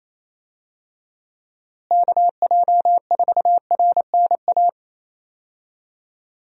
Morse code, heard as a keyed tone, about 28 wpm, 700 Hz. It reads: KJ4RNA
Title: USA Call Signs - 6 Characters - Repeated 2x - 28wpm